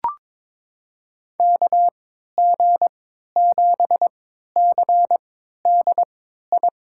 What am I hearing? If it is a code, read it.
Morse code, heard as a keyed tone, about 22 wpm, spaced as for 12 wpm, 700 Hz. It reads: KG7CDI
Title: USA Call Signs - 6 Characters - 3x Character Spacing - 22wpm